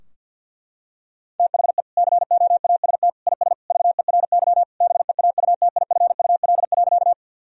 Morse code, heard as a keyed tone, about 50 wpm, 700 Hz. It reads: THE COAST IS VERY BEAUTIFUL.
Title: Sentences from Top 1000 Words - Review Part 4 - 50wpm